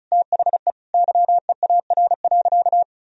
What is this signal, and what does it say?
Morse code, heard as a keyed tone, about 34 wpm, 700 Hz. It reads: THE YEAR.
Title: Sentences from Top 1000 Words - Review Part 1 - 35wpm